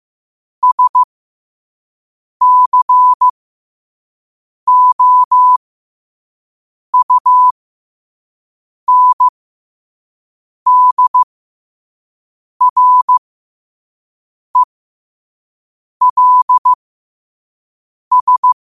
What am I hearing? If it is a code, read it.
Morse code, heard as a keyed tone, about 15 wpm, spaced as for 5 wpm, 1000 Hz. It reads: SCOUNDRELS
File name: qotdfiles / 5wpm